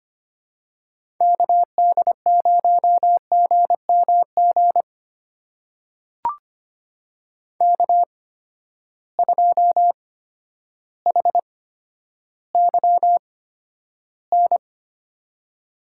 Morse code, heard as a keyed tone, about 25 wpm, spaced as for 11 wpm, 700 Hz. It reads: KD0GMG K2HYN
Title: USA Call Signs - 8x Character Spacing - 25wpm